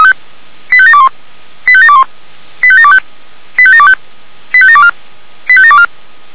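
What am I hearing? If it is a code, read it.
Five-tone selective call (ZVEI), heard as quick runs of five tones, tones of about 70 ms. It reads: 85721 85721 85725 85725 85724 85724
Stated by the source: Five-tone ZVEI selective call received on SDR.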